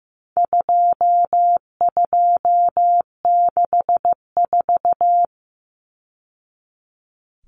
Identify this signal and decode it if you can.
Morse code, heard as a keyed tone, about 15 wpm, 700 Hz. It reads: 2264
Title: Numbers - 4-Digit 15wpm